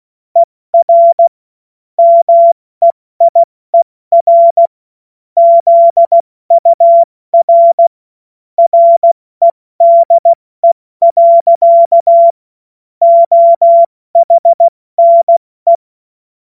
Morse code, heard as a keyed tone, about 16 wpm, 680 Hz. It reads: ER MEIER ZUR REDE. OHNE
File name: Witz-07-Oktober-2025-70-F80